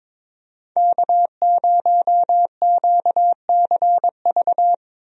Morse code, heard as a keyed tone, about 21 wpm, 700 Hz. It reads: K0QCV
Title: USA Call Signs - 5 Characters - 22wpm